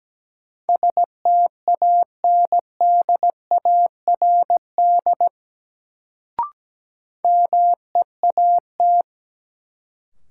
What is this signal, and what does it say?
Morse code, heard as a keyed tone, about 17 wpm, 700 Hz. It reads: STANDARD MEAT
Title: Sentences from Top 1200 Words - New Word Drill Alt - 17wpm